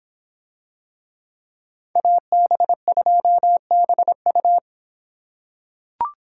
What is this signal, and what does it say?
Morse code, heard as a keyed tone, about 26 wpm, 700 Hz. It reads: AB2BU